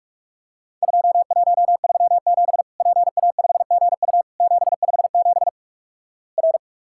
Morse code, heard as a keyed tone, about 44 wpm, 700 Hz. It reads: 2137 PA5GU 757 R
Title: Worked All Europe DX Contest - QTC Exchange - 45wpm